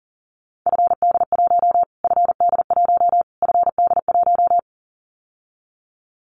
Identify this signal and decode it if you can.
Morse code, heard as a keyed tone, about 40 wpm, 700 Hz. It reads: FD1 FD1 FD1
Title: Callsign Prefixes for Top 25 DXCC - Repeated 3x - 40wpm